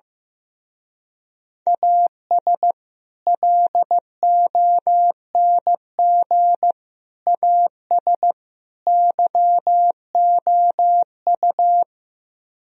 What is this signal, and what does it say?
Morse code, heard as a keyed tone, about 15 wpm, 700 Hz. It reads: AS LONG AS YOU
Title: Speed-Racing - Sets of 4 Words 15wpm